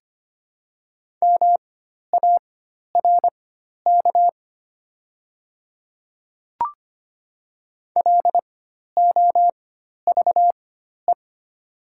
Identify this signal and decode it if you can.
Morse code, heard as a keyed tone, about 25 wpm, spaced as for 12 wpm, 700 Hz. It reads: MARK LOVE
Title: 4x Character Spacing - Top 500 Words - 25wpm